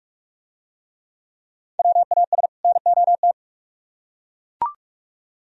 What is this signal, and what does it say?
Morse code, heard as a keyed tone, about 41 wpm, 700 Hz. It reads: WAS NOT